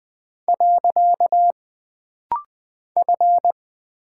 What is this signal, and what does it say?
Morse code, heard as a keyed tone, about 20 wpm, 700 Hz. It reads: . F
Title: Single Character (Letter:Number:Punctuation:Prosign) - Rapid-Fire - 20wpm